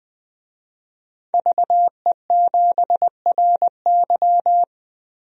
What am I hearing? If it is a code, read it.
Morse code, heard as a keyed tone, about 21 wpm, 700 Hz. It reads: VE7RY
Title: DXCC-100 Call Signs - 5 Characters - 20wpm